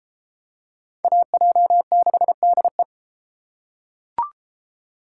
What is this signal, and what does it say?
Morse code, heard as a keyed tone, about 33 wpm, 700 Hz. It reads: AJ6DE